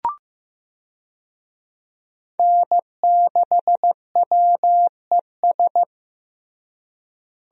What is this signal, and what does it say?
Morse code, heard as a keyed tone, about 15 wpm, 700 Hz. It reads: N6WES
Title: USA Call Signs - 5 Characters - Repeated 3x - 15wpm